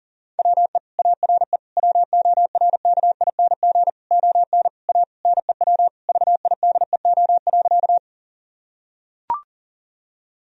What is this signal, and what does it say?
Morse code, heard as a keyed tone, about 40 wpm, 700 Hz. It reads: WE ARE WORKING ON A NEW VIDEO.
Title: Sentences from Top 1000 Words - Review Part 4 - 40wpm